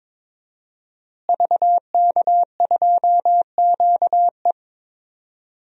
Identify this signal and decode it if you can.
Morse code, heard as a keyed tone, about 22 wpm, 700 Hz. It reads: VK2QE
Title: DXCC-100 Call Signs - 5 Characters - Repeated 3x - 22wpm